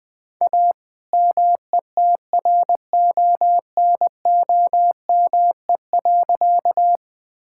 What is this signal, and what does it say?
Morse code, heard as a keyed tone, about 20 wpm, 700 Hz. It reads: A METRONOME.
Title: All-Age Appropriate Jokes - 20wpm